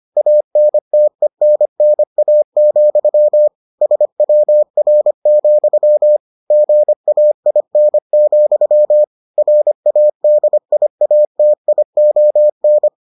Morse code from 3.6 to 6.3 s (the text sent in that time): SWR,